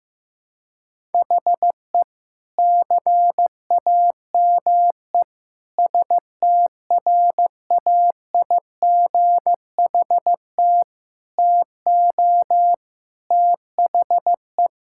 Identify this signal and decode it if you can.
Morse code, heard as a keyed tone, about 16 wpm, 700 Hz. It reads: HE CAME STRAIGHT TO THE
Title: Sentences from Top 1000 Words - Review Part 6 - 15wpm